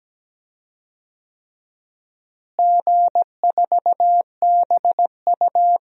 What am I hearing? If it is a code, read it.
Morse code, heard as a keyed tone, about 17 wpm, 700 Hz. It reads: G4BU
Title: DXCC-100 Call Signs - 4 Characters - Repeated 3x - 17wpm